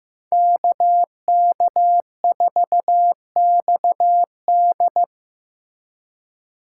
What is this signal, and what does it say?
Morse code, heard as a keyed tone, about 18 wpm, 700 Hz. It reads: KK4XD
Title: Speed Racing - USA 2x2 Call Signs 15wpm